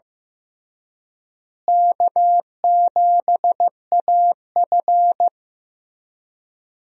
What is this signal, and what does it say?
Morse code, heard as a keyed tone, about 15 wpm, 700 Hz. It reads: K7AF
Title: USA Call Signs - 4 Characters - Repeated 3x - 15wpm